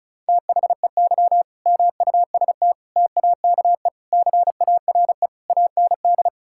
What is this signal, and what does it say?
Morse code, heard as a keyed tone, about 34 wpm, 700 Hz. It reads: THEY MUST TAKE CARE AND
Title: Sentences from Top 1000 Words - Review Part 6 - 35wpm